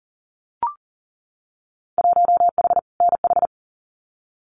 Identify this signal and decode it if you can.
Morse code, heard as a keyed tone, about 41 wpm, 700 Hz. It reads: 1H NH
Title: Winter Field Day Exchanges 40wpm